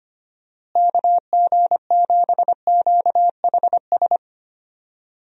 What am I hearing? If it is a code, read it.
Morse code, heard as a keyed tone, about 25 wpm, 700 Hz. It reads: KG7QHS
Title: USA Call Signs - 6 Characters - Repeated 2x - 25wpm